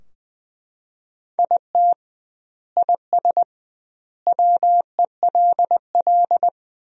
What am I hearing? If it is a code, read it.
Morse code, heard as a keyed tone, about 20 wpm, 700 Hz. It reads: IT IS WELL
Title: 2x Word-Spacing - Sentences from Top 300 Words - 20wpm